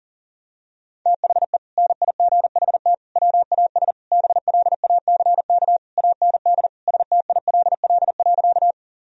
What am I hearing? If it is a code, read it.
Morse code, heard as a keyed tone, about 37 wpm, 700 Hz. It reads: THE NIGHT WAS BLACK AND STILL.
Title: Sentences from Top 1000 Words - Review Part 2 - 40wpm